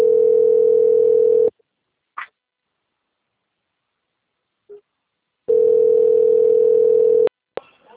Ringback tone (North America), 440 + 480 Hz.
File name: out- - - - - .44428